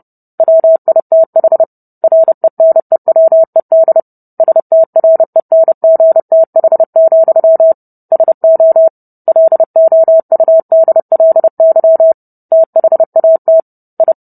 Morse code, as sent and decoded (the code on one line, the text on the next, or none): .-- .. - .... / .-. -. .-- . -.. / ... - .-. -. --. - .... --..-- / ... --- / .-.. --- ..- -.. .-.. -.-- / - .... .- - / ..
WITH RENEWED STRENGTH, SO LOUDLY THAT I